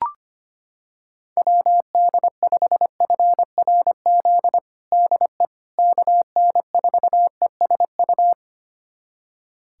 Morse code, heard as a keyed tone, about 25 wpm, 700 Hz. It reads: WD5FRZ DE KN4ESU
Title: USA Callsign DE Callsign 25wpm